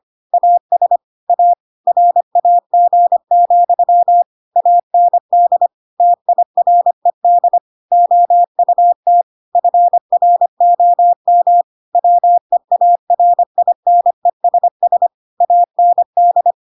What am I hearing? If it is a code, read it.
Morse code, heard as a keyed tone, about 25 wpm, 700 Hz. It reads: AS A RAG, AND TIRED OUT FROM WEARINESS AND